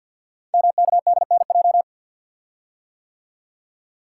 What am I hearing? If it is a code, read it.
Morse code, heard as a keyed tone, about 48 wpm, 700 Hz. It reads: MKDNJ
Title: Code Groups - 5 Letters - 50wpm